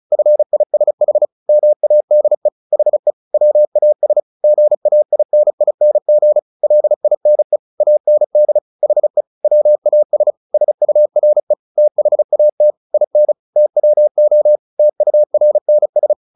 Morse code, heard as a keyed tone, about 34 wpm, 600 Hz. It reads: FISH MADE HE WAS GAINING LINE AND HE WAS SURE THAT IN TWO TURNS
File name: the old man and the sea 2 35wpm